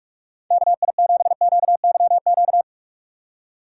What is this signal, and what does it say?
Morse code, heard as a keyed tone, about 46 wpm, 700 Hz. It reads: KI7QYQ
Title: USA Call Signs - 6 Characters - 2x Character Spacing - 45wpm